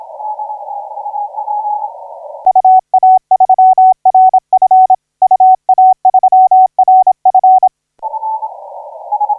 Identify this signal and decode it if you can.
Morse code, heard as a keyed tone, about 26 wpm, 750 Hz. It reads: UA3RF UA3RF
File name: 9G5AA 1712UTC 12m